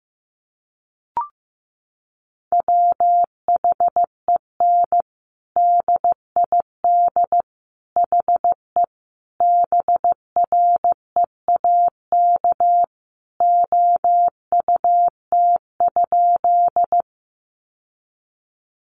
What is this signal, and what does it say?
Morse code, heard as a keyed tone, about 15 wpm, 700 Hz. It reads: WHEN DID HE BREAK OUT?
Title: Sentences from Top 900 Words 15wpm